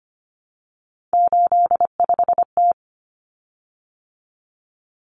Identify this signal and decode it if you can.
Morse code, heard as a keyed tone, about 25 wpm, 700 Hz. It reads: 85T
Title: Common Cut Numbers - 3-Digit 25wpm